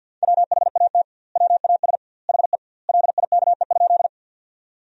Morse code, heard as a keyed tone, about 50 wpm, 700 Hz. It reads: WHAT WAS HE LIKE?
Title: 2x Word-Spacing - Sentences from Top 100 Words - 50wpm